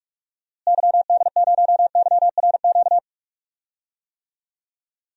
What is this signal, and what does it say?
Morse code, heard as a keyed tone, about 41 wpm, 700 Hz. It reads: YD0YRQ
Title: DXCC-100 Call Signs - 6 Characters - Repeated 3x - 45wpm